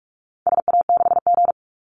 Morse code, heard as a keyed tone, about 44 wpm, 700 Hz. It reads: SA6G
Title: Call Signs Repeated 3x - 45wpm